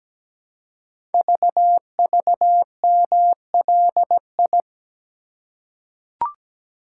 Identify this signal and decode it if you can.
Morse code, heard as a keyed tone, about 17 wpm, 700 Hz. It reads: VVMLI
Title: Code Groups - 5 Letters - 17wpm